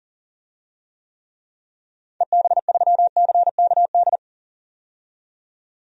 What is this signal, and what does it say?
Morse code, heard as a keyed tone, about 38 wpm, 700 Hz. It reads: EB3CKD
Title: DXCC-100 Call Signs - 6 Characters - 40wpm